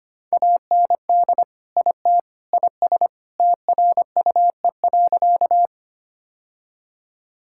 Morse code, heard as a keyed tone, about 25 wpm, 700 Hz. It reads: AND IT IS TRUE.
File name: practice / 25wpm / Sentences with 3-Word Phrases - 25wpm